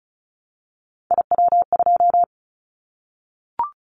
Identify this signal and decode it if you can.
Morse code, heard as a keyed tone, about 34 wpm, 700 Hz. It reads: IW2
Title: Callsign Prefixes for Top 25 DXCC - 35wpm